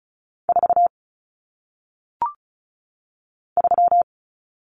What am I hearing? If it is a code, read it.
Morse code, heard as a keyed tone, about 36 wpm, 700 Hz. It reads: 4 3
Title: Numbers - 1-Digit 35wpm